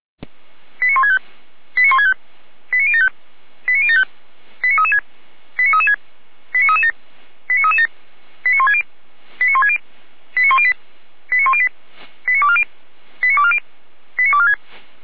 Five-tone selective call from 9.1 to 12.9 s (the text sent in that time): 89160 89108 89108 89240